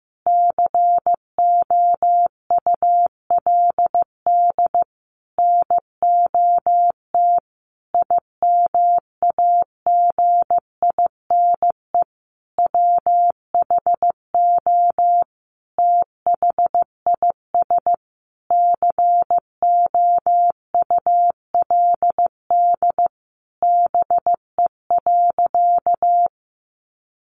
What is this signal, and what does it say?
Morse code, heard as a keyed tone, about 15 wpm, 700 Hz. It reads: COULD NOT IMAGINE WHO THIS COULD BE.
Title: Sentences from Top 1400 Words 15wpm